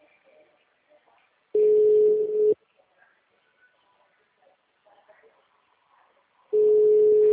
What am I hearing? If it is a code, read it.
Ringback tone (Europe, 425 Hz).
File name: out- -2013- - - .1694